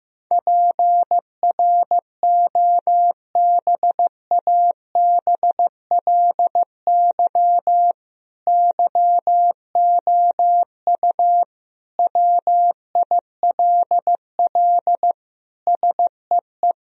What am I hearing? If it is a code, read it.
Morse code, heard as a keyed tone, about 15 wpm, 700 Hz. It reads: PROBABLY YOU WILL SEE
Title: Sentences from Top 1000 Words - Review Part 4 - 15wpm